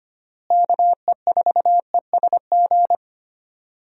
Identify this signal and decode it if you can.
Morse code, heard as a keyed tone, about 25 wpm, 700 Hz. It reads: KE4ESG
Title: USA Call Signs - 6 Characters - Repeated 3x - 25wpm